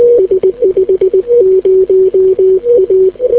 Morse code, heard as a keyed tone, about 20 wpm, 370 Hz. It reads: S50A